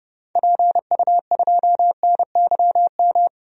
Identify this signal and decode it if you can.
Morse code, heard as a keyed tone, about 30 wpm, 700 Hz. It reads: PU2NYM
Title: DXCC-100 Call Signs - 6 Characters - 30wpm